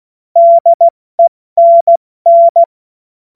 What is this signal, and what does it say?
Morse code, heard as a keyed tone, about 16 wpm, 680 Hz. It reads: DENN
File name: Morsekurs / Witze / Witz 04. Januar 2022, 70 F80